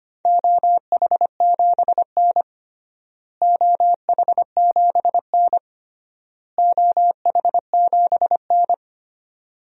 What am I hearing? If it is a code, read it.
Morse code, heard as a keyed tone, about 25 wpm, 700 Hz. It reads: OH7N OH7N OH7N
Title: DXCC-100 Call Signs - 4 Characters - Repeated 3x - 25wpm